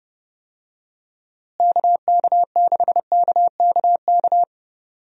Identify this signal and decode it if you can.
Morse code, heard as a keyed tone, about 30 wpm, 700 Hz. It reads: KK6KKK
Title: USA Call Signs - 6 Characters - Repeated 3x - 30wpm